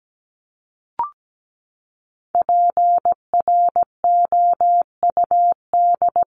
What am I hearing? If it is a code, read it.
Morse code, heard as a keyed tone, about 17 wpm, 700 Hz. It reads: PROUD
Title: Top 1501-1750 Words 17wpm